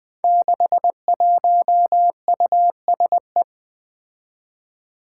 Morse code, heard as a keyed tone, about 20 wpm, 700 Hz. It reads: 61USE
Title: Five Challenging Letter-Number Combinations 20wpm